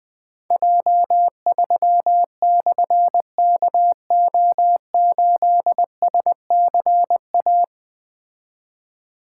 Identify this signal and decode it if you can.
Morse code, heard as a keyed tone, about 20 wpm, 700 Hz. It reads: J3/KO8SCA
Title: Difficult Call Signs 20wpm